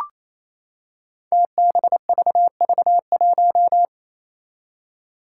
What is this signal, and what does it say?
Morse code, heard as a keyed tone, about 28 wpm, 700 Hz. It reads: TBVV1
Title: Five Challenging Letter-Number Combinations 28wpm